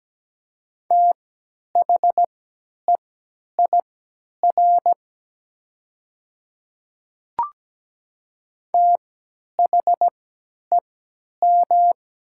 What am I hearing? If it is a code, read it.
Morse code, heard as a keyed tone, about 17 wpm, spaced as for 10 wpm, 700 Hz. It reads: THEIR THEM
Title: 3x Character Spacing - Top 100 Words - 17wpm